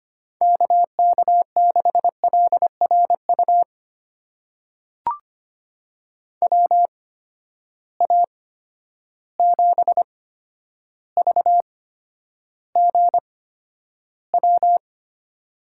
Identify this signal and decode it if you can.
Morse code, heard as a keyed tone, about 25 wpm, spaced as for 11 wpm, 700 Hz. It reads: KK6LRU WA7VGW
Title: USA Call Signs - 6 Characters - 8x Character Spacing - 25wpm